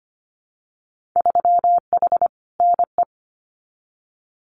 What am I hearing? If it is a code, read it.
Morse code, heard as a keyed tone, about 25 wpm, 690 Hz. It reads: 3H NE
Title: Winter Field Day Exchanges Encore 25wpm